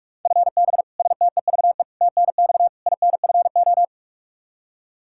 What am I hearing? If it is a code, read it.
Morse code, heard as a keyed tone, about 42 wpm, 700 Hz. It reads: FB STEVE TNX INFO
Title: Practice QSO - Exchange 3 - More Info - 45wpm